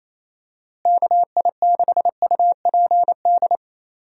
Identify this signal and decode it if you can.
Morse code, heard as a keyed tone, about 28 wpm, 700 Hz. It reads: KI6UPD
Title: USA Call Signs - 6 Characters - Repeated 3x - 28wpm